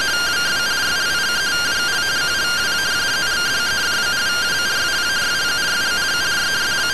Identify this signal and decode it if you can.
Two-tone teleprinter signal, about 170 Hz shift.